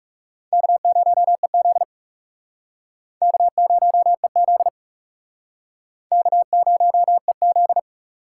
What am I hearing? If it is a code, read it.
Morse code, heard as a keyed tone, about 36 wpm, 700 Hz. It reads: K0EZ K0EZ K0EZ
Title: Speed Racing - USA 1x2 Call Signs 30wpm